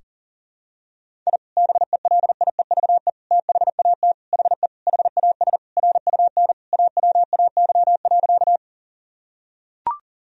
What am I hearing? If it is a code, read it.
Morse code, heard as a keyed tone, about 40 wpm, 700 Hz. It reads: I BELIEVE THAT HE HAS RUN AWAY.
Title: Sentences from Top 1000 Words - Review Part 1 - 40wpm